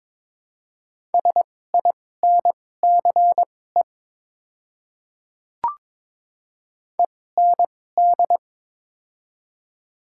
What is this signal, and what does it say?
Morse code, heard as a keyed tone, about 22 wpm, spaced as for 16 wpm, 700 Hz. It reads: SINCE END
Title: 2x Character Spacing - Top 300 Words - 22wpm